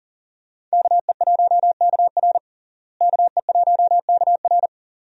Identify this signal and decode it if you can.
Morse code, heard as a keyed tone, about 40 wpm, 700 Hz. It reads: KE1KR KE1KR
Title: USA Call Signs - 5 Characters - Repeated 2x - 40wpm